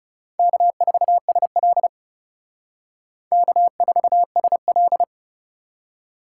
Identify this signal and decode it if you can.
Morse code, heard as a keyed tone, about 36 wpm, 700 Hz. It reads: K4SL K4SL
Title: Speed Racing - USA 1x2 Call Signs 30wpm